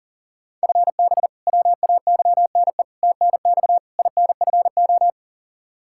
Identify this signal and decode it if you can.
Morse code, heard as a keyed tone, about 40 wpm, 700 Hz. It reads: FB WAYNE TNX INFO
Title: Practice QSO - Exchange 3 - More Info - 40wpm